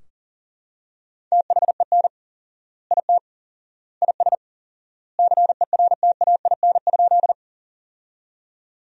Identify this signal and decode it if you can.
Morse code, heard as a keyed tone, about 40 wpm, 700 Hz. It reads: THEN IT IS CERTAIN?
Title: 4x Word-Spacing - Sentences from Top 500 Words - 40wpm